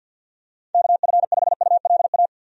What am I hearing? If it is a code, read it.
Morse code, heard as a keyed tone, about 51 wpm, 700 Hz. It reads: KF5ULA
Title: USA Call Signs - 6 Characters - 50wpm